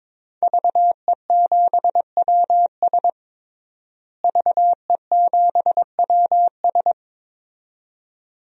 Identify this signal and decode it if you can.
Morse code, heard as a keyed tone, about 22 wpm, 700 Hz. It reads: VE7WS VE7WS
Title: DXCC-100 Call Signs - 5 Characters - Repeated 2x - 22wpm